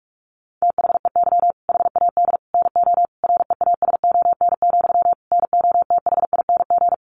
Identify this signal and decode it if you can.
Morse code, heard as a keyed tone, about 45 wpm, 700 Hz. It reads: THEY HAD NO REASON, NOTHING
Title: Sentences from Top 1500 Words 45wpm